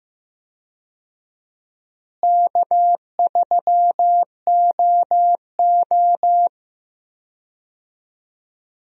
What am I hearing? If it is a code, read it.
Morse code, heard as a keyed tone, about 15 wpm, 700 Hz. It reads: K3OO
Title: USA Call Signs - 4 Characters - Repeated 3x - 15wpm